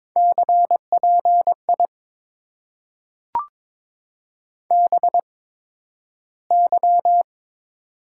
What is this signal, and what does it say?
Morse code, heard as a keyed tone, about 22 wpm, spaced as for 12 wpm, 690 Hz. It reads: CPI BY
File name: Top 90 QSO Elements - 8x Character Spacing - 22wpm